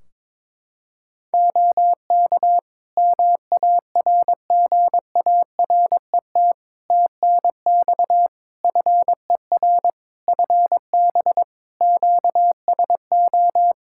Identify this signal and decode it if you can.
Morse code, heard as a keyed tone, about 22 wpm, 700 Hz. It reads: OK MARGARET TNX FER FB QSO
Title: Practice QSO - Exchange 4 - Ending - 22wpm